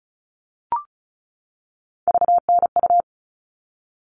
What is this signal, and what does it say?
Morse code, heard as a keyed tone, about 34 wpm, 700 Hz. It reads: VNU
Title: All Cut Numbers - 3-Digit 35wpm